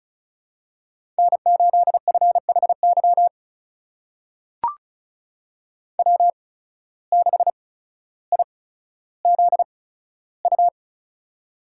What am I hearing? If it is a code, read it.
Morse code, heard as a keyed tone, about 36 wpm, spaced as for 15 wpm, 700 Hz. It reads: N8FHY W6IZU
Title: USA Call Signs - 5 Characters - 8x Character Spacing - 35wpm